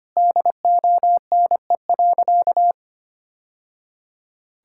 Morse code, heard as a keyed tone, about 25 wpm, 700 Hz. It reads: DONE.
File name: practice / 25wpm / Sentences with 3-Word Phrases - 25wpm